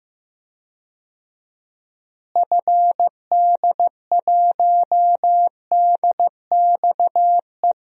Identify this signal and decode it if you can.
Morse code, heard as a keyed tone, about 15 wpm, 700 Hz. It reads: FD1DXE
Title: DXCC-100 Call Signs - 6 Characters - Repeated 3x - 15wpm